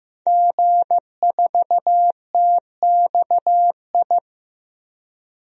Morse code, heard as a keyed tone, about 15 wpm, 700 Hz. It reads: G4TXI
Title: DXCC-100 Call Signs - 5 Characters - 15wpm